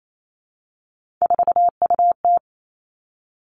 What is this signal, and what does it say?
Morse code, heard as a keyed tone, about 28 wpm, 700 Hz. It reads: 4UT
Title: Three Challenging Letter-Number Combinations 28wpm